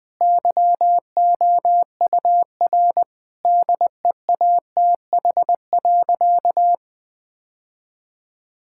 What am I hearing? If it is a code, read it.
Morse code, heard as a keyed tone, about 21 wpm, 700 Hz. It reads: YOUR DEATH.
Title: Sentences from Top 1600 Words 20wpm